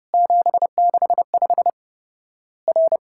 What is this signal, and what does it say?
Morse code, heard as a keyed tone, about 30 wpm, 700 Hz. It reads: 765 R
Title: Worked All Europe DX Contest - QTC Exchange - 30wpm